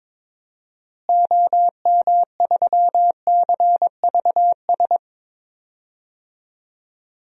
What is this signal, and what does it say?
Morse code, heard as a keyed tone, about 22 wpm, 700 Hz. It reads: OM3CVS
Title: DXCC-100 Call Signs - 6 Characters - Repeated 3x - 22wpm